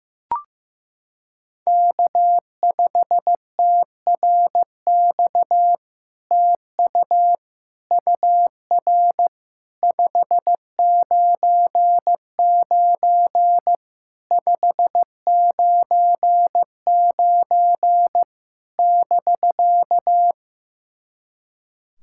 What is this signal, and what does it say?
Morse code, heard as a keyed tone, about 15 wpm, 700 Hz. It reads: K5TRX TU UR 599 599 <BK>
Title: SOTA QSO Practice 15wpm